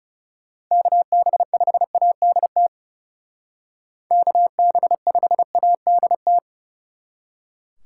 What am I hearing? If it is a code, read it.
Morse code, heard as a keyed tone, about 36 wpm, 700 Hz. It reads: KB5ADT KB5ADT
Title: Speed Racing - USA 2x3 Call Signs 30wpm